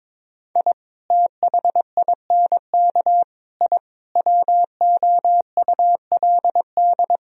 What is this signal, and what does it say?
Morse code, heard as a keyed tone, about 22 wpm, 700 Hz. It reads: I THINK I WOULD
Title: Sentences from Top 1000 Words - Review Part 4 - 22wpm